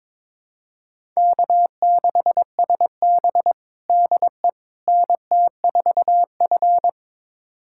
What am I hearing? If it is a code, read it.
Morse code, heard as a keyed tone, about 21 wpm, 700 Hz. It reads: K6SB DE NT4F
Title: USA Callsign DE Callsign - 4 Characters - 22wpm